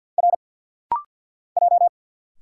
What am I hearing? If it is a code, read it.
Morse code, heard as a keyed tone, about 50 wpm, 700 Hz. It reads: R J
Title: Single Letters - Rapid-Fire 50wpm